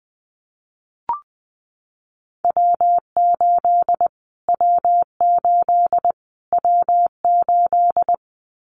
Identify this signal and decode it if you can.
Morse code, heard as a keyed tone, about 20 wpm, 700 Hz. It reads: W8 W8 W8
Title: Callsign Prefixes for Top 25 DXCC - Repeated 3x - 20wpm